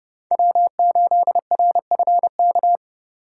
Morse code, heard as a keyed tone, about 30 wpm, 700 Hz. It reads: W8RFK